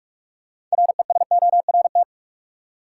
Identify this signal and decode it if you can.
Morse code, heard as a keyed tone, about 46 wpm, 700 Hz. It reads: RESORT
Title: Six Letter Words - 45wpm